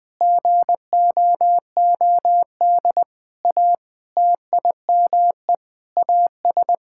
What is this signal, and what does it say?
Morse code, heard as a keyed tone, about 21 wpm, 700 Hz. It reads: GOOD A TIME AS
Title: Sentences with 7-Word Phrases - 20wpm